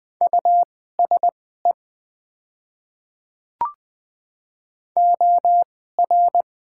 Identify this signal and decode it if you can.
Morse code, heard as a keyed tone, about 20 wpm, spaced as for 14 wpm, 690 Hz. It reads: USE OR
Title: 2x Character Spacing - Top 100 Words - 20wpm